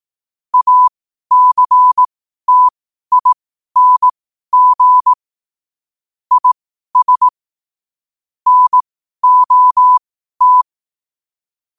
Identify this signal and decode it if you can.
Morse code, heard as a keyed tone, about 18 wpm, spaced as for 12 wpm, 1000 Hz. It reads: ACTING IS NOT
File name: qotdfiles / 12wpm